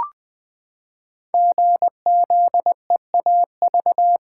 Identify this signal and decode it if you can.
Morse code, heard as a keyed tone, about 20 wpm, 700 Hz. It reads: GZEAV
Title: Code Groups - 5 Letter:Numbers - 20wpm